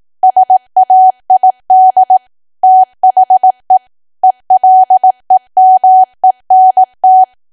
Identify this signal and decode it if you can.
Morse code, heard as a keyed tone, about 18 wpm, 750 Hz. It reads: SAID THE ELEMENT
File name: cwa / W1AW-Level2 / 18WPM